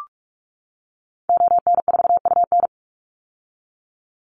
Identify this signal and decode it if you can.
Morse code, heard as a keyed tone, about 44 wpm, 700 Hz. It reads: ON4UN
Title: Call Signs - 45wpm